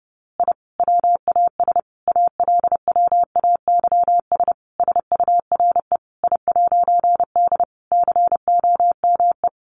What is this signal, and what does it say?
Morse code, heard as a keyed tone, about 30 wpm, 700 Hz. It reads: I WAS ALWAYS SURE I'D COME
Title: A Sound Of Thunder 30wpm